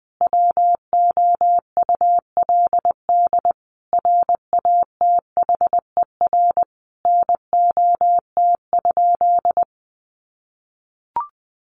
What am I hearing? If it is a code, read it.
Morse code, heard as a keyed tone, about 20 wpm, 700 Hz. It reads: WOULD RATHER NOT?
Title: Sentences from Top 700 Words 20wpm